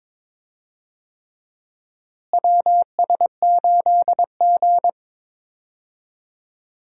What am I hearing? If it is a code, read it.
Morse code, heard as a keyed tone, about 22 wpm, 700 Hz. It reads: WS8G
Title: USA Call Signs - 4 Characters - Repeated 3x - 22wpm